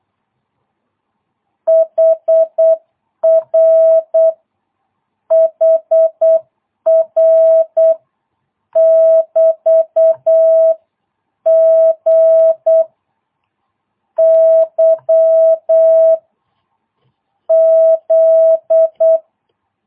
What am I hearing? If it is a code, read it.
Morse code, heard as a keyed tone, about 8 wpm, 650 Hz. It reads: HR HR <BT> G Y Z